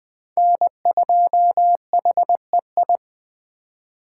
Morse code, heard as a keyed tone, about 20 wpm, 700 Hz. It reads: N2HEI
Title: Speed Racing - USA 1x3 Call Signs 17wpm